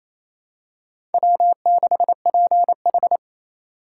Morse code, heard as a keyed tone, about 28 wpm, 700 Hz. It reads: W6PH
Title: USA Call Signs - 4 Characters - 2x Character Spacing - 28wpm